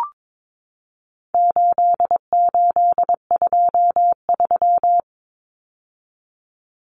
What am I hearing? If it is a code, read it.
Morse code, heard as a keyed tone, about 22 wpm, 700 Hz. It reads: 8823
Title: Numbers - 4-Digit 22wpm